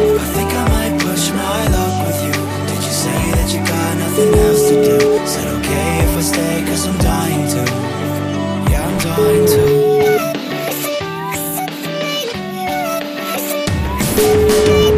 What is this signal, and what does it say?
Ringback tone (Europe, 425 Hz), with noise.